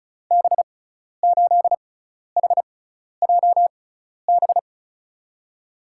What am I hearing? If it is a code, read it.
Morse code, heard as a keyed tone, about 36 wpm, spaced as for 12 wpm, 700 Hz. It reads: B8HJB
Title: USA Call Signs - 6 Characters - 6x Character Spacing - 35wpm